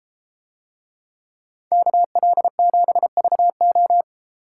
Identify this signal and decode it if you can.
Morse code, heard as a keyed tone, about 33 wpm, 700 Hz. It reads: KL7VO